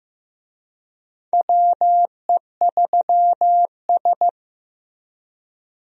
Morse code, heard as a keyed tone, about 15 wpm, 700 Hz. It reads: WE3S
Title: USA Call Signs - 4 Characters - Repeated 2x - 15wpm